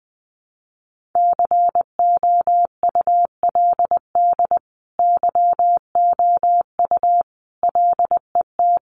Morse code, heard as a keyed tone, about 19 wpm, 700 Hz. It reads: COULD YOU LET
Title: Sentences from Top 1400 Words 20wpm